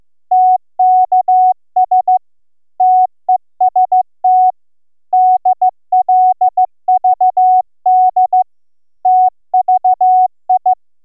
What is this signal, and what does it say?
Morse code, heard as a keyed tone, about 14 wpm, 740 Hz. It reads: TKS TEST DLVD TVI